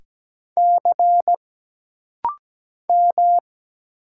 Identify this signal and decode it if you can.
Morse code, heard as a keyed tone, about 17 wpm, 700 Hz. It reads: C M
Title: Single Letters - Rapid-Fire 17wpm